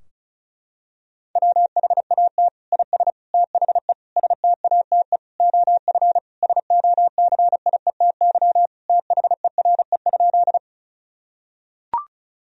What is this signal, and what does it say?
Morse code, heard as a keyed tone, about 34 wpm, 700 Hz. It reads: WHAT IS THE STATE OF SOCIETY THERE?
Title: Sentences from Top 1000 Words - Review Part 4 - 35wpm